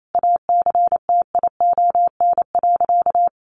Morse code, heard as a keyed tone, about 28 wpm, 700 Hz. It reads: ACTION.
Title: Sentences from Top 700 Words 28wpm